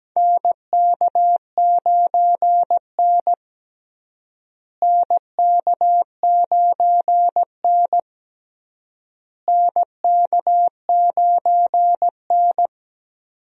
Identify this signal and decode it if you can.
Morse code, heard as a keyed tone, about 17 wpm, 700 Hz. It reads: NK9N NK9N NK9N
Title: USA Call Signs - 4 Characters - Repeated 3x - 17wpm